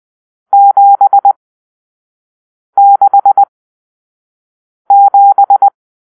Morse code, heard as a keyed tone, about 20 wpm, spaced as for 5 wpm, 800 Hz. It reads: 767